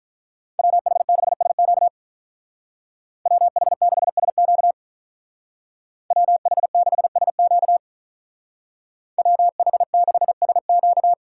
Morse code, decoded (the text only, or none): WH6SQ WH6SQ WH6SQ WH6SQ